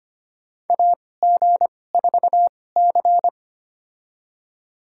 Morse code, heard as a keyed tone, about 25 wpm, spaced as for 18 wpm, 700 Hz. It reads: AG4C
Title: USA Call Signs - 2x Character Spacing - 25wpm